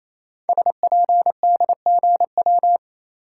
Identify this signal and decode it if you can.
Morse code, heard as a keyed tone, about 29 wpm, 700 Hz. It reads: SPDGW
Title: Code Groups - 5 Letters - 28wpm